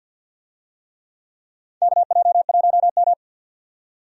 Morse code, heard as a keyed tone, about 52 wpm, 700 Hz. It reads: KJ1M
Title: USA Call Signs - 4 Characters - 50wpm